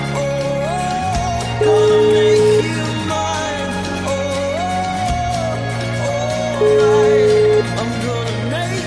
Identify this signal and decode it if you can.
Ringback tone (Europe, 425 Hz), with noise.